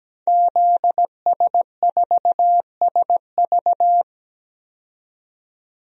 Morse code, heard as a keyed tone, about 17 wpm, 700 Hz. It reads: ZS4SV
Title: DXCC-100 Call Signs - 5 Characters - 17wpm